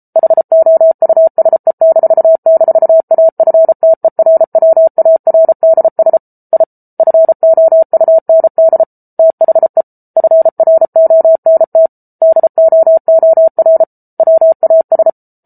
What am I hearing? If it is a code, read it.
Morse code, heard as a keyed tone, about 34 wpm, 650 Hz. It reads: HOUSE--AFTERWARDS I FOUND THE FRONT DOOR WAS